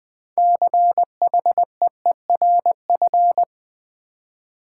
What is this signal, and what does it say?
Morse code, heard as a keyed tone, about 20 wpm, 690 Hz. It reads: CHEERF
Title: Build A Word - 3 Letters At A Time - 8-Letter Words - 20wpm